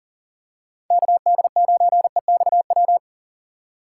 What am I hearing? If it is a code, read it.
Morse code, heard as a keyed tone, about 40 wpm, 700 Hz. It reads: KD9EXW
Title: USA Call Signs - 6 Characters - 3x Character Spacing - 40wpm